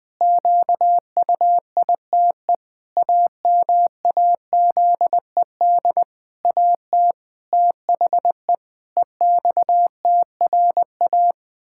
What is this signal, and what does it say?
Morse code, heard as a keyed tone, about 20 wpm, 700 Hz. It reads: QUITE AMAZED AT THE EXTRA
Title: Pangram Challenge v2 - 20wpm